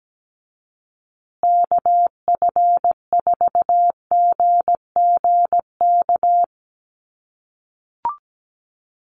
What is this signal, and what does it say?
Morse code, heard as a keyed tone, about 17 wpm, 700 Hz. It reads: KF4GGK
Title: USA Call Signs Repeated 2x 17wpm